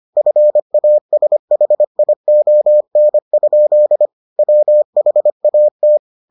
Morse code, heard as a keyed tone, about 25 wpm, 600 Hz. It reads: FASHION? WHAT